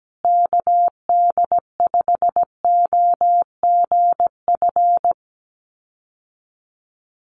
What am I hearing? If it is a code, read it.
Morse code, heard as a keyed tone, about 17 wpm, 700 Hz. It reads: KD5OGF
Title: USA Call Signs Repeated 2x 17wpm